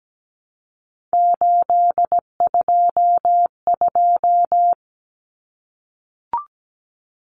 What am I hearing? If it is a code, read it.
Morse code, heard as a keyed tone, about 17 wpm, 700 Hz. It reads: 822
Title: Numbers - 3-Digit 17wpm